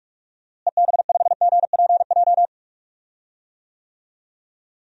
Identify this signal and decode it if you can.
Morse code, heard as a keyed tone, about 46 wpm, 700 Hz. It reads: EB5GPJ
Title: DXCC-100 Call Signs - 6 Characters - Repeated 2x - 45wpm